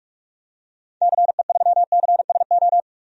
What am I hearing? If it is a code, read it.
Morse code, heard as a keyed tone, about 46 wpm, 700 Hz. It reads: CE3CSO
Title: DXCC-100 Call Signs - 6 Characters - 45wpm